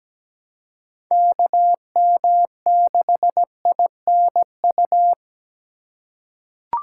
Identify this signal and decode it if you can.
Morse code, heard as a keyed tone, about 17 wpm, 700 Hz. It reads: KM6INU
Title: USA Call Signs - 3x Character Spacing - 17wpm